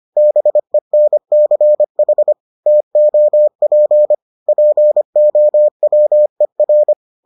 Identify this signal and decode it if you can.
Morse code, heard as a keyed tone, about 25 wpm, 600 Hz. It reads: BENCH TOP POWER